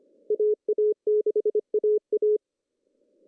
Morse code, heard as a keyed tone, about 25 wpm, 420 Hz. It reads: AA6AA
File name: CW / cqww-cw-k5tr